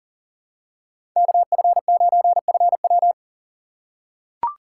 Morse code, heard as a keyed tone, about 40 wpm, 700 Hz. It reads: KF9FW
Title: USA Call Signs - 2x Character Spacing - 40wpm